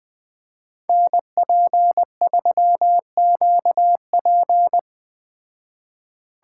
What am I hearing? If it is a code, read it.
Morse code, heard as a keyed tone, about 21 wpm, 700 Hz. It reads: NP3QP